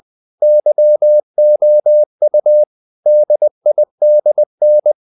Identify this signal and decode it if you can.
Morse code, heard as a keyed tone, about 20 wpm, 600 Hz. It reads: YOU DIDN
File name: the old man and the sea / the old man and the sea part 1 20wpm